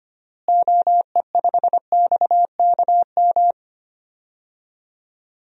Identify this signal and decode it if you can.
Morse code, heard as a keyed tone, about 25 wpm, 700 Hz. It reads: OE5XKM
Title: DXCC-100 Call Signs - 6 Characters - Repeated 3x - 25wpm